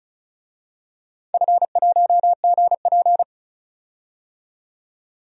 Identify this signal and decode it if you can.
Morse code, heard as a keyed tone, about 36 wpm, 690 Hz. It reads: F1GP
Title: DXCC-100 Call Signs - 4 Characters - 35wpm